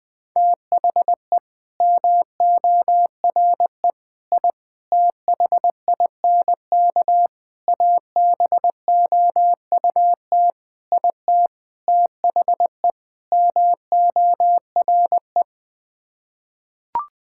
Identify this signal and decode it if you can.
Morse code, heard as a keyed tone, about 21 wpm, 700 Hz. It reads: THE MORE I THINK ABOUT IT THE MORE
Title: Sets of 8 Words - 20wpm